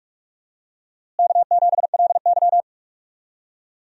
Morse code, heard as a keyed tone, about 46 wpm, 700 Hz. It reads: K7LY
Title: USA Call Signs - 4 Characters - Repeated 3x - 45wpm